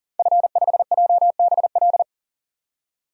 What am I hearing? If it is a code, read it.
Morse code, heard as a keyed tone, about 40 wpm, 700 Hz. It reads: F5JBL
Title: DXCC-100 Call Signs - 5 Characters - 40wpm